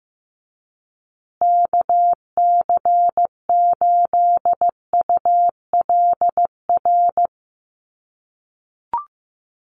Morse code, heard as a keyed tone, about 15 wpm, 700 Hz. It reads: KC8ULR
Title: USA Call Signs Repeated 2x 15wpm